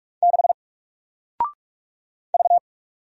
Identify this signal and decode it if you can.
Morse code, heard as a keyed tone, about 46 wpm, 700 Hz. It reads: 6 V
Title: Single Character (Letter:Number:Punctuation:Prosign) Alt 45wpm